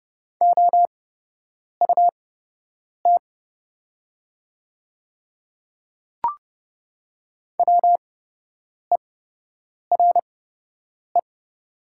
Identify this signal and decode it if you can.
Morse code, heard as a keyed tone, about 30 wpm, spaced as for 8 wpm, 700 Hz. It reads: OUT WERE
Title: 8x Character Spacing - Top 200 Words - 30wpm